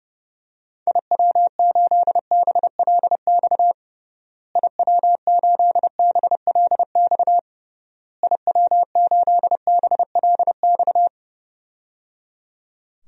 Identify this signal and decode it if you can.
Morse code, heard as a keyed tone, about 30 wpm, 700 Hz. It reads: IW8BLX IW8BLX IW8BLX
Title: DXCC-100 Call Signs - 6 Characters - Repeated 3x - 30wpm